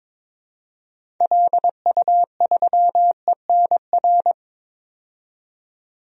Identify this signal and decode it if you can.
Morse code, heard as a keyed tone, about 22 wpm, 700 Hz. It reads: LU3ENR
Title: DXCC-100 Call Signs - 6 Characters - 22wpm